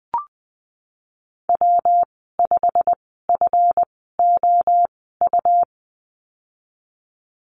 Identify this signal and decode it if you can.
Morse code, heard as a keyed tone, about 19 wpm, spaced as for 14 wpm, 700 Hz. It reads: W5FOU
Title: 193 Lesson 31g - Letter v - Call Signs 20wpm